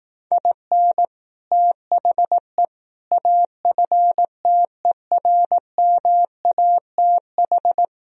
Morse code, heard as a keyed tone, about 18 wpm, 700 Hz. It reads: IN THE AFTERMATH
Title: Speed-Racing - Sets of 5 Words 15wpm